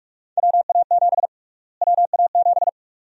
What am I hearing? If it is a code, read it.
Morse code, heard as a keyed tone, about 46 wpm, 690 Hz. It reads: WA7 WA7
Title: USA Call Sign Prefixes - Repeated 2x - 45wpm